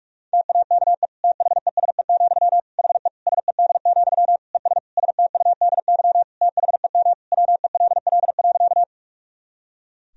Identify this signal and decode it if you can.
Morse code, heard as a keyed tone, about 46 wpm, 700 Hz. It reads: TAKE THESE, HE SED, ES STUDY THEM WELL.
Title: Sentences from Top 300 Words and Abbreviations 45wpm